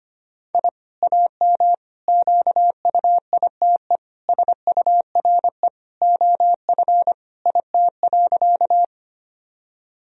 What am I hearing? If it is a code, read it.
Morse code, heard as a keyed tone, about 25 wpm, 700 Hz. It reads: I AM QUITE SURE OF IT.
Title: Sentences from Top 1000 Words - Review Part 2 - 25wpm